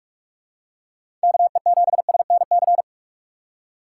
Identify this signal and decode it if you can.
Morse code, heard as a keyed tone, about 46 wpm, 700 Hz. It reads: KE7SNC
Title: USA Call Signs - 6 Characters - 2x Character Spacing - 45wpm